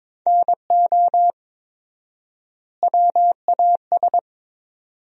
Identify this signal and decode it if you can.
Morse code, heard as a keyed tone, about 21 wpm, 700 Hz. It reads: NO WAS
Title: 4x Word-Spacing - Sentences from Top 100 Words - 22wpm